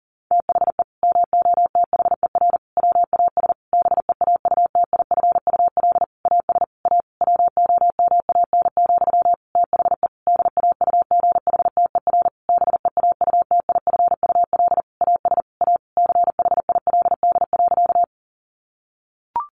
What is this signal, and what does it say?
Morse code, heard as a keyed tone, about 40 wpm, 700 Hz. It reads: THE MOTHER WAS BEAUTIFUL AS A WOMAN, THE DAUGHTER BEAUTIFUL AS A CHILD.
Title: Sentences from Top 1500 Words 40wpm